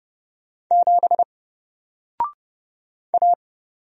Morse code, heard as a keyed tone, about 30 wpm, 700 Hz. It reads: 7 A
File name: Single-Letter-Number-30wpm